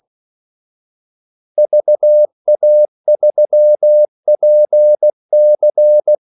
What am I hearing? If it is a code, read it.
Morse code, heard as a keyed tone, about 16 wpm, 600 Hz. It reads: VA3PC
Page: Kaldesignaler 16 wpm